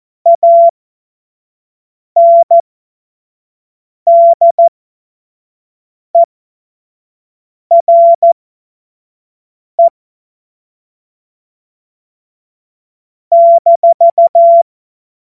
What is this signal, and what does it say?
Morse code, heard as a keyed tone, about 14 wpm, spaced as for 5 wpm, 680 Hz. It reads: ANDERE -
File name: Morsekurs / Witze / Witz 13. Dezember 2022 25 F70